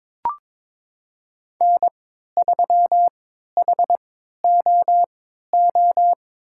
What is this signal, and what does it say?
Morse code, heard as a keyed tone, about 22 wpm, spaced as for 13 wpm, 700 Hz. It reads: N3HOO
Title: USA Call Signs - 5 Characters - 3x Character Spacing - 22wpm